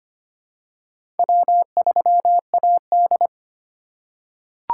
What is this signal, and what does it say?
Morse code, heard as a keyed tone, about 25 wpm, 700 Hz. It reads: W3AD
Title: USA Call Signs - 4 Characters - 2x Character Spacing - 25wpm